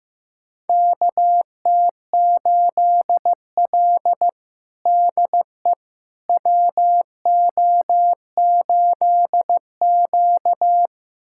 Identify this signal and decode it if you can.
Morse code, heard as a keyed tone, about 15 wpm, 700 Hz. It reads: KT8L DE WO8Q
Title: USA Callsign DE Callsign - 4 Characters - 15wpm